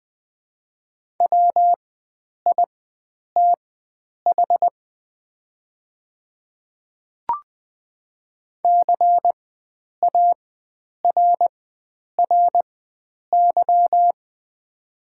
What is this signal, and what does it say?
Morse code, heard as a keyed tone, about 20 wpm, spaced as for 9 wpm, 700 Hz. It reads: WITH CARRY
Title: 4x Character Spacing - Top 300 Words - 20wpm